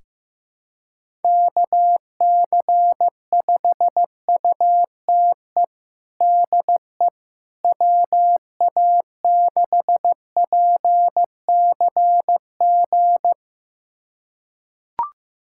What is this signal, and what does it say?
Morse code, heard as a keyed tone, about 15 wpm, 700 Hz. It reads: KC5UTE DE WA6PCG
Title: USA Callsign DE Callsign 15wpm